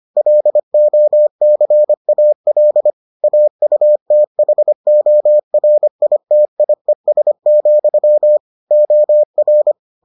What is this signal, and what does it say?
Morse code, heard as a keyed tone, about 25 wpm, 600 Hz. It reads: LOCAL AUTHORITIES, OR